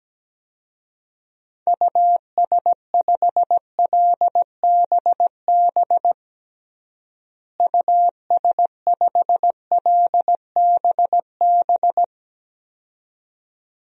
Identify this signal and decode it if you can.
Morse code, heard as a keyed tone, about 17 wpm, 700 Hz. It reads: US5LBB US5LBB
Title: DXCC-100 Call Signs - 6 Characters - Repeated 2x - 17wpm